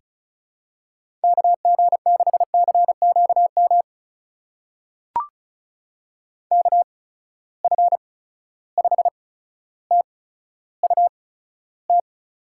Morse code, heard as a keyed tone, about 36 wpm, spaced as for 15 wpm, 700 Hz. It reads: KG6CQM KF5TUT
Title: USA Call Signs - 8x Character Spacing - 35wpm